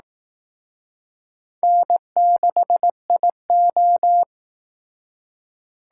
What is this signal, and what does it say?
Morse code, heard as a keyed tone, about 19 wpm, 700 Hz. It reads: N6IO